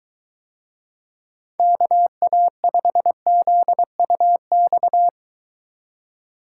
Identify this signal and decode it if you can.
Morse code, heard as a keyed tone, about 23 wpm, 700 Hz. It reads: KA5ZUX